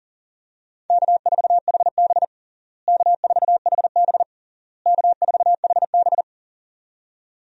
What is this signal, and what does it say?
Morse code, heard as a keyed tone, about 40 wpm, 700 Hz. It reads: K4HB K4HB K4HB
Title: USA Call Signs - 4 Characters - Repeated 3x - 40wpm